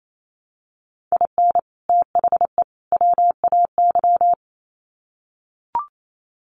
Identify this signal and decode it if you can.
Morse code, heard as a keyed tone, about 28 wpm, 700 Hz. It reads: IN THE WAY
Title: Sets of 3 Words 28wpm